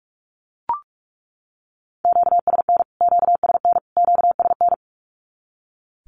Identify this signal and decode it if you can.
Morse code, heard as a keyed tone, about 44 wpm, 700 Hz. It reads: QSN QSN QSN
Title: Q-Codes Repeated 3x 45wpm